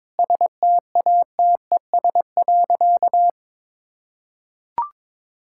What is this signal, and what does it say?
Morse code, heard as a keyed tone, about 22 wpm, 700 Hz. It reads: STATES.
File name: practice / 22wpm / Sentences with 3-Word Phrases - 22wpm